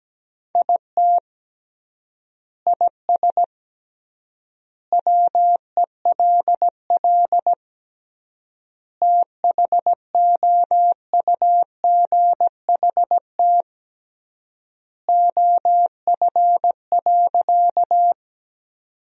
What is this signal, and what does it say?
Morse code, heard as a keyed tone, about 17 wpm, 700 Hz. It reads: IT IS WELL THOUGHT OF.
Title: 3x Word-Spacing - Sentences from Top 300 Words - 17wpm